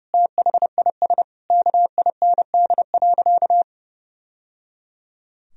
Morse code, heard as a keyed tone, about 30 wpm, 700 Hz. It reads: THIS KIND.
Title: Sentences from Top 1000 Words - Review Part 4 - 30wpm